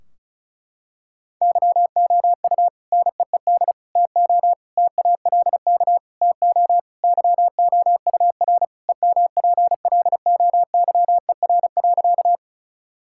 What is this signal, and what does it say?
Morse code, heard as a keyed tone, about 36 wpm, 700 Hz. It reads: YOU NEED TO TALK TO YOUR EMPLOYER.
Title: Sentences from Top 1600 Words 35wpm